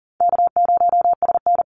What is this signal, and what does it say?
Morse code, heard as a keyed tone, about 38 wpm, 700 Hz. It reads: K0SN
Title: Easy Call Signs - 40wpm